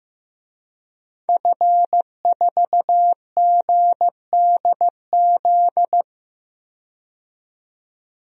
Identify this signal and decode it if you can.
Morse code, heard as a keyed tone, about 15 wpm, 700 Hz. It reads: F4GDZ
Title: DXCC-100 Call Signs - 5 Characters - 15wpm